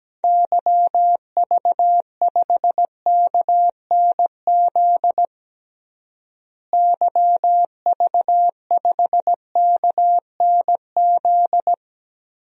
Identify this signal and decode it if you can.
Morse code, heard as a keyed tone, about 17 wpm, 700 Hz. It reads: YV5KNZ YV5KNZ
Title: DXCC-100 Call Signs - 6 Characters - Repeated 3x - 17wpm